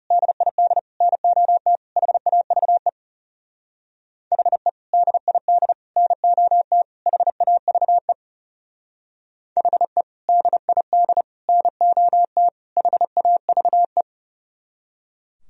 Morse code, heard as a keyed tone, about 35 wpm, 700 Hz. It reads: DID NOT HAVE HE DID NOT HAVE HE DID NOT HAVE
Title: Speed-Racing - Sets of 4 Words 30wpm